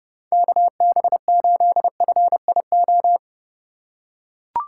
Morse code, heard as a keyed tone, about 32 wpm, 700 Hz. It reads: KB8FIO
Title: USA Call Signs - 6 Characters - 3x Character Spacing - 30wpm